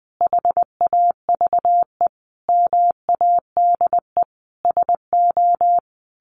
Morse code, heard as a keyed tone, about 20 wpm, 700 Hz. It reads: HAVE MADE SO
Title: Sentences from Top 1400 Words 20wpm